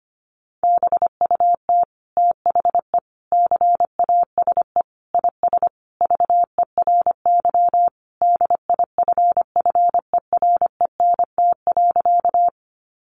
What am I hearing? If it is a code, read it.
Morse code, heard as a keyed tone, about 25 wpm, 700 Hz. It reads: BUT THE CASE IS VERY DIFFERENT.
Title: Sentences from Top 700 Words 25wpm